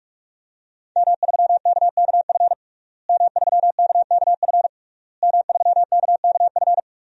Morse code, heard as a keyed tone, about 46 wpm, 700 Hz. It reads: M3KKF M3KKF M3KKF
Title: DXCC-100 Call Signs - 5 Characters - Repeated 3x - 45wpm